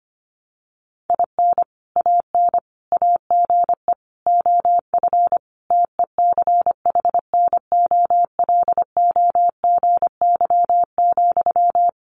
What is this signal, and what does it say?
Morse code, heard as a keyed tone, about 25 wpm, 700 Hz. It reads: IN AN AGE OF TECHNOLOGY,
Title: Sentences from Top 1200 Words 25wpm